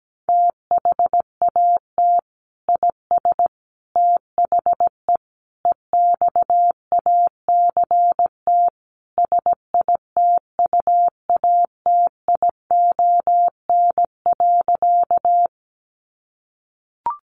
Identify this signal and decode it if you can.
Morse code, heard as a keyed tone, about 17 wpm, 700 Hz. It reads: THAT IS THE EXACT SITUATION.
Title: Sentences from Top 1300 Words 17wpm